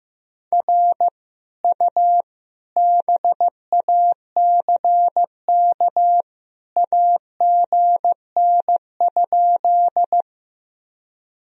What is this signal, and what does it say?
Morse code, heard as a keyed tone, about 16 wpm, 700 Hz. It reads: R U BACK AGN?
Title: Sentences from Top 200 Words and Abbreviations 15wpm